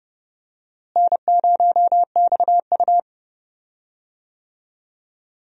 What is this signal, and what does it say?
Morse code, heard as a keyed tone, about 31 wpm, 700 Hz. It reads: N0XU
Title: USA Call Signs - 4 Characters - Repeated 2x - 30wpm